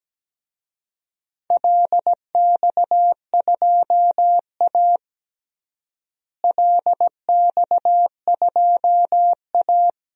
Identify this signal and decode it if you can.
Morse code, heard as a keyed tone, about 17 wpm, 700 Hz. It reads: LX2A LX2A
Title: DXCC-100 Call Signs - 4 Characters - Repeated 2x - 17wpm